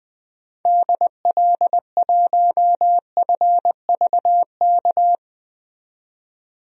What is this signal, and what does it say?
Morse code, heard as a keyed tone, about 20 wpm, 700 Hz. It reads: DL1FVK
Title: DXCC-100 Call Signs - 6 Characters - 20wpm